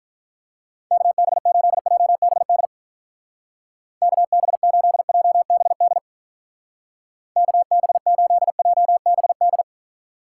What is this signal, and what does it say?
Morse code, heard as a keyed tone, about 46 wpm, 700 Hz. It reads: KB8JBD KB8JBD KB8JBD